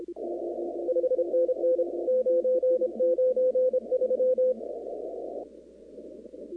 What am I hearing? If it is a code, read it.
Morse code, heard as a keyed tone, about 26 wpm, 510 Hz. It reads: 5NN 993
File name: contests / 2019-cw-wpx-kt5j-k5pi- -2330 0-44